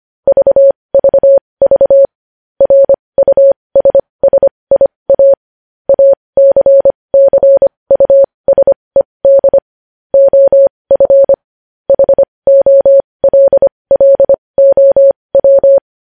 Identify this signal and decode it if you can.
Morse code, heard as a keyed tone, about 25 wpm, spaced as for 20 wpm, 550 Hz. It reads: VVV RUSSIA ACCUSED OF HOLLOW
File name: podcast / News headlines-1x-25x20